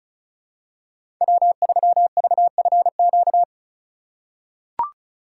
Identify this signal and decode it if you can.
Morse code, heard as a keyed tone, about 36 wpm, 700 Hz. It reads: W3VFQ
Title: USA Call Signs - 5 Characters - 4x Character Spacing - 35wpm